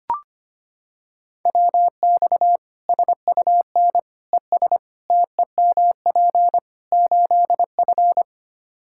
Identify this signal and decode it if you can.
Morse code, heard as a keyed tone, about 25 wpm, 700 Hz. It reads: WX SUN ES TEMP 8F
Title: Practice QSO - Exchange 2 - Info - 25wpm